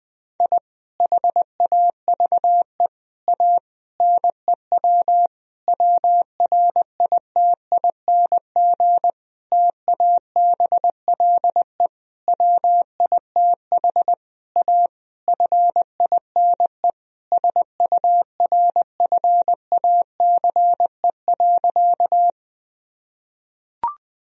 Morse code, heard as a keyed tone, about 20 wpm, 700 Hz. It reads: I HAVE A NEW WRITING TABLE WITH A FINE SURFACE.
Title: Sentences from Top 1000 Words - Review Part 3 - 20wpm